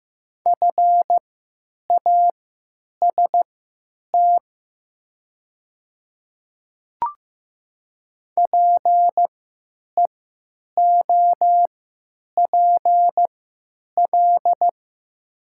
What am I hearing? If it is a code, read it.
Morse code, heard as a keyed tone, about 15 wpm, spaced as for 9 wpm, 700 Hz. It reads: FAST PEOPL
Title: 3x Character Spacing - Top 500 Words - 15wpm